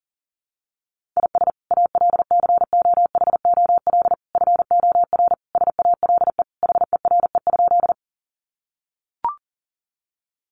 Morse code, heard as a keyed tone, about 40 wpm, 700 Hz. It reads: IS ALCOHOL FOR SALE HERE?
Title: Sentences from Top 1400 Words 40wpm